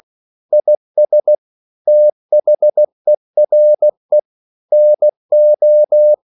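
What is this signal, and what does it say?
Morse code, heard as a keyed tone, about 16 wpm, 600 Hz. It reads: IS THERE NO
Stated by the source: English quotations sent in CW as practice at 16 words per minute.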